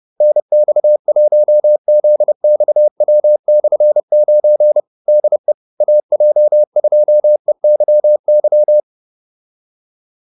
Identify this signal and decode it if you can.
Morse code, heard as a keyed tone, about 30 wpm, 600 Hz. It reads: NX1ZXW/9 DE AJ2EYY
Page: advanced-practice-files